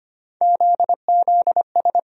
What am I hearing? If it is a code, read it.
Morse code, heard as a keyed tone, about 25 wpm, 700 Hz. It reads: ZZS
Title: USA Call Sign Postfixes - 3 Letters 25wpm